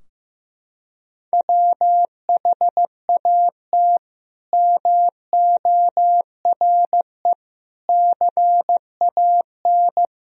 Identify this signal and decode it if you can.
Morse code, heard as a keyed tone, about 16 wpm, 700 Hz. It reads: WHAT MORE CAN
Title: Sentences from Top 1000 Words - Review Part 4 - 15wpm